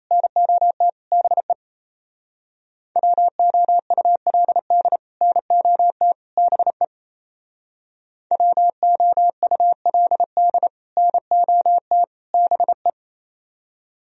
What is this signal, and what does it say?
Morse code, heard as a keyed tone, about 34 wpm, 700 Hz. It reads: NOT BE WOULD NOT BE WOULD NOT BE
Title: Speed-Racing - Sets of 3 Words 28wpm